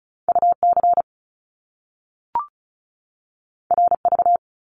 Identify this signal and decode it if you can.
Morse code, heard as a keyed tone, about 36 wpm, 700 Hz. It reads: UC RV
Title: Top Two Letter Combinations in Words 35wpm